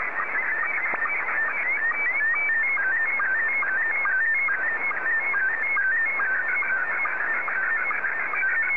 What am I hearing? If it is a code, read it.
SSTV (slow-scan television), Scottie 1.